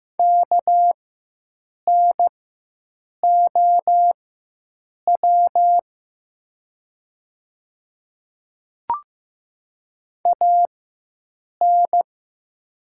Morse code, heard as a keyed tone, about 15 wpm, spaced as for 7 wpm, 700 Hz. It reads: KNOW AN
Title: 4x Character Spacing - Top 100 Words - 15wpm